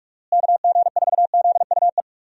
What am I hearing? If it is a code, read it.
Morse code, heard as a keyed tone, about 41 wpm, 700 Hz. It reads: KG4ZUE
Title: USA Call Signs - 6 Characters - Repeated 2x - 45wpm